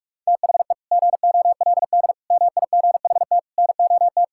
Morse code, heard as a keyed tone, about 46 wpm, 700 Hz. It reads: THE GOLD MIGHT NOT
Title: Sentences from Top 1000 Words - Review Part 5 - 45wpm